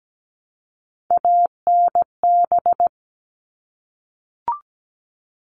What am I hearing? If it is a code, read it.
Morse code, heard as a keyed tone, about 17 wpm, 700 Hz. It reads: ANB
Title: All Cut Numbers - 3-Digit 17wpm